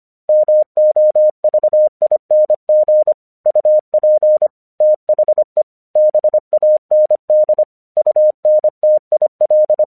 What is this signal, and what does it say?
Morse code, heard as a keyed tone, about 26 wpm, 610 Hz. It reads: MOVING UP THE BAND UNTIL
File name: 1 English 25wpm / AQC29